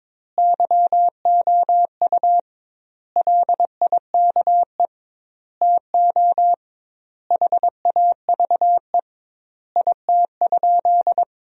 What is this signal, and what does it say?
Morse code, heard as a keyed tone, about 22 wpm, 700 Hz. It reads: YOU LIKE TO HAVE IT?
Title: 2x Word-Spacing - Sentences from Top 100 Words - 22wpm